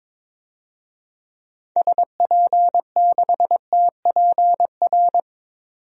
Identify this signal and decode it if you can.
Morse code, heard as a keyed tone, about 22 wpm, 700 Hz. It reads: SP6TPR
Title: DXCC-100 Call Signs - 6 Characters - 22wpm